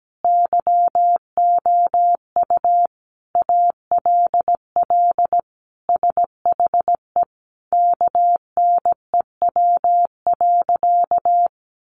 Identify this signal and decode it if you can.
Morse code, heard as a keyed tone, about 17 wpm, 700 Hz. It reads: YOU ALL SHE KNEW.
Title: Sentences from Top 1300 Words 17wpm